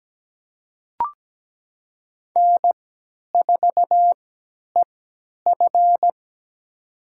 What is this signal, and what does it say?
Morse code, heard as a keyed tone, about 17 wpm, spaced as for 10 wpm, 700 Hz. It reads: N4EF
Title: USA Call Signs - 4 Characters - 3x Character Spacing - 17wpm